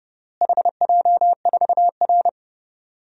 Morse code, heard as a keyed tone, about 30 wpm, 700 Hz. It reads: HJ4R
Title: DXCC-100 Call Signs - 4 Characters - Repeated 2x - 30wpm